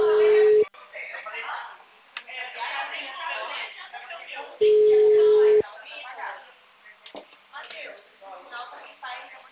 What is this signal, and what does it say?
Ringback tone (Europe, 425 Hz), with speech.